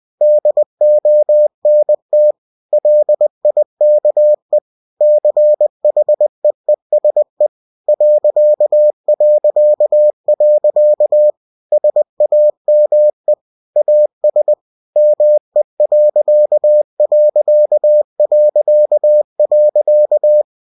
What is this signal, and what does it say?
Morse code, heard as a keyed tone, about 20 wpm, 600 Hz. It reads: DONT LIKE CHEESE ... SAME AS ME....
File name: cw-books / winnie-the-pooh / winnie-20wpm-page-0037